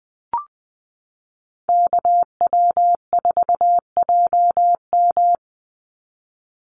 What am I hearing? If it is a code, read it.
Morse code, heard as a keyed tone, about 20 wpm, 700 Hz. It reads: KW4JM
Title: NA SOTA Activator Callsigns 20wpm